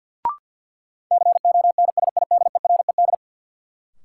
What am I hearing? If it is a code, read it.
Morse code, heard as a keyed tone, about 52 wpm, 700 Hz. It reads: CONSIDERED
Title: Sets of 5 Words - New Word Drill - 50wpm